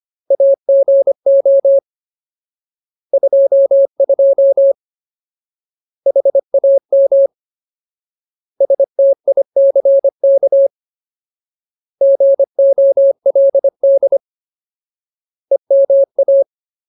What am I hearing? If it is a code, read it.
Morse code, heard as a keyed tone, about 25 wpm, 550 Hz. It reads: AGO 22 HAM STICK GOLD EMA
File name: Ultimate Word List / 25WPM Ultimate